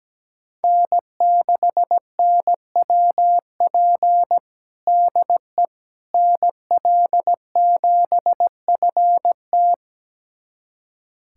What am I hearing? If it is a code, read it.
Morse code, heard as a keyed tone, about 17 wpm, 700 Hz. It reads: N6NWP DE NL7FT
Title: USA Callsign DE Callsign - 5 Characters - 17wpm